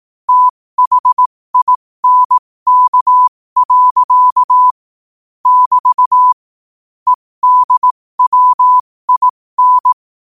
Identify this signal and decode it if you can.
Morse code, heard as a keyed tone, about 18 wpm, spaced as for 15 wpm, 1000 Hz. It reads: THINK. <BT> EDWIN
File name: qotdfiles / 15wpm